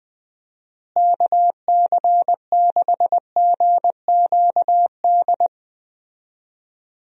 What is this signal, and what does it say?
Morse code, heard as a keyed tone, about 21 wpm, 700 Hz. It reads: KC6GQD
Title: USA Call Signs - 6 Characters - Repeated 2x - 20wpm